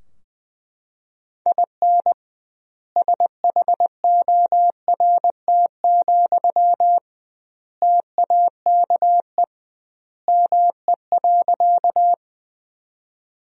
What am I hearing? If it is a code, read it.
Morse code, heard as a keyed tone, about 20 wpm, 700 Hz. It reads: IN SHORT, TAKE ME.
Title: 2x Word-Spacing - Sentences from Top 400 Words - 20wpm